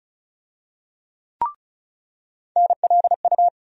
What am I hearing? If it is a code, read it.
Morse code, heard as a keyed tone, about 36 wpm, 700 Hz. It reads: NLU
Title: USA Call Sign Postfixes - 3 Letters 35wpm